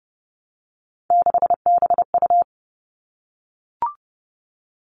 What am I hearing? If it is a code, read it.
Morse code, heard as a keyed tone, about 30 wpm, 700 Hz. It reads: 6BU
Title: All Cut Numbers - 3-Digit 30wpm